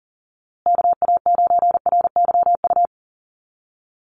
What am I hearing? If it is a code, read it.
Morse code, heard as a keyed tone, about 40 wpm, 700 Hz. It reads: KA9RYU
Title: USA Call Signs Repeated 2x 40wpm